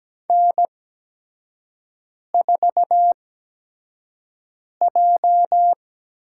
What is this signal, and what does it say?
Morse code, heard as a keyed tone, about 17 wpm, spaced as for 5 wpm, 700 Hz. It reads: N4J
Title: USA Call Signs - 5 Characters - 8x Character Spacing - 17wpm